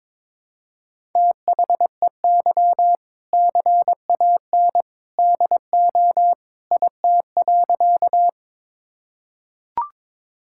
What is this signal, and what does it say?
Morse code, heard as a keyed tone, about 22 wpm, 700 Hz. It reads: THEY CAN DO IT.
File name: practice / 22wpm / Sentences with 2-Word Phrases - 22wpm